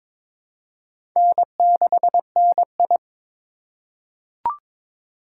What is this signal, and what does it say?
Morse code, heard as a keyed tone, about 22 wpm, 700 Hz. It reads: N6NI
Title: USA Call Signs - 4x Character Spacing - 22wpm